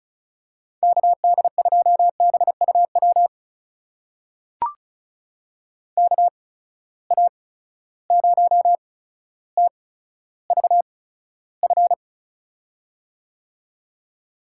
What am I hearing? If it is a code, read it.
Morse code, heard as a keyed tone, about 34 wpm, spaced as for 15 wpm, 700 Hz. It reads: KD2BUW KA0TVF
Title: USA Call Signs - 6 Characters - 8x Character Spacing - 35wpm